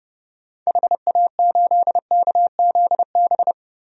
Morse code, heard as a keyed tone, about 30 wpm, 700 Hz. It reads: HA8KZB
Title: DXCC-100 Call Signs - 6 Characters - 30wpm